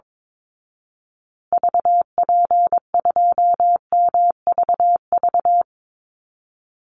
Morse code, heard as a keyed tone, about 22 wpm, 700 Hz. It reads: VP2MVV
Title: Call Signs Repeated 2x - 22wpm